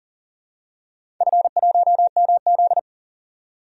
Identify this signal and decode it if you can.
Morse code, heard as a keyed tone, about 40 wpm, 700 Hz. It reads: F1MZ
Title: DXCC-100 Call Signs - 4 Characters - 40wpm